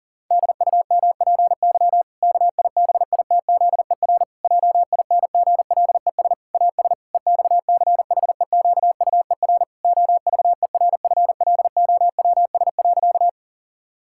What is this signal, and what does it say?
Morse code, heard as a keyed tone, about 37 wpm, 700 Hz. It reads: DUMPY KIBITZER JINGLES AS EXCHEQUER OVERFLOWS.
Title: Pangram Challenge v2 - 40wpm